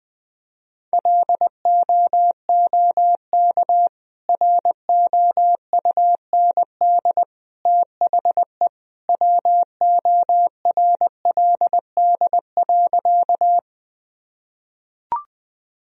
Morse code, heard as a keyed tone, about 21 wpm, 700 Hz. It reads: LOOK ROUND THE WORLD.
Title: Sentences from Top 1000 Words - Review Part 1 - 20wpm